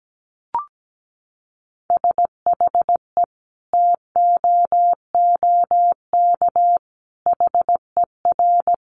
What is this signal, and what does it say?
Morse code, heard as a keyed tone, about 17 wpm, 700 Hz. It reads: SHE TOOK HER
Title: Sentences from Top 900 Words 17wpm